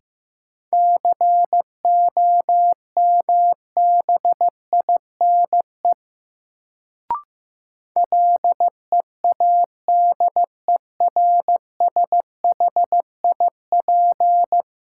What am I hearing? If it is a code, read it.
Morse code, heard as a keyed tone, about 15 wpm, 700 Hz. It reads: COMBINE LEADERSHIP
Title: Sentences from Top 1600 Words - New Word Drill - 15wpm